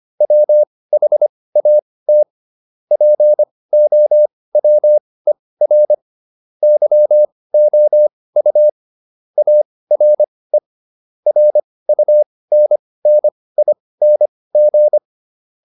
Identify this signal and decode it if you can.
Morse code, heard as a keyed tone, about 25 wpm, spaced as for 18 wpm, 600 Hz. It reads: WHAT POWER YOU ARE RUNNING